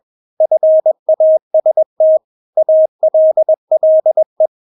Morse code, heard as a keyed tone, about 21 wpm, 630 Hz. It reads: FAST ALLE
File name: Witz2 21wpm